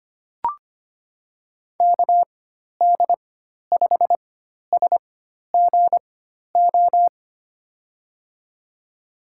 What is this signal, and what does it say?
Morse code, heard as a keyed tone, about 25 wpm, spaced as for 12 wpm, 700 Hz. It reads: KD5SGO
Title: USA Call Signs - 6 Characters - 4x Character Spacing - 25wpm